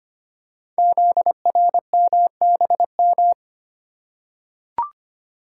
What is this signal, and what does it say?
Morse code, heard as a keyed tone, about 25 wpm, 700 Hz. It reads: ZRMBM
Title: Code Groups - 5 Letters - 25wpm